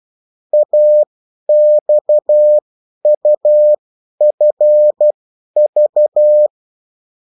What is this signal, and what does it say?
Morse code, heard as a keyed tone, about 12 wpm, spaced as for 10 wpm, 600 Hz. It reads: AXUFV
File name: morse mp3 / revision9-vitesse 10 mots